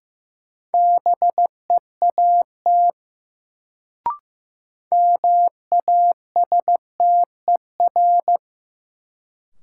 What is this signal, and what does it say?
Morse code, heard as a keyed tone, about 15 wpm, 700 Hz. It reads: BEAT MASTER
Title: Sentences from Top 1300 Words - New Word Drill Alt - 15wpm